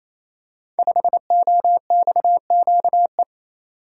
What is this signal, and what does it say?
Morse code, heard as a keyed tone, about 27 wpm, 700 Hz. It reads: 5OXQE
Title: Code Groups - 5 Letter:Numbers - 28wpm